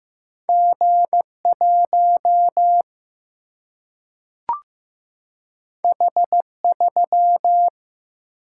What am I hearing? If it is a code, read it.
Morse code, heard as a keyed tone, about 15 wpm, 700 Hz. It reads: G1 H3
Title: Single-Letter Single-Number 15wpm